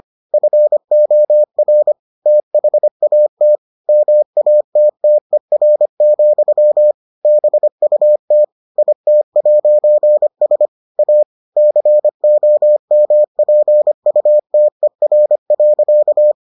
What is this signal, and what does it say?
Morse code, heard as a keyed tone, about 25 wpm, 600 Hz. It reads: FOR THAT MATTER, BUT IT'S A COMPUTER.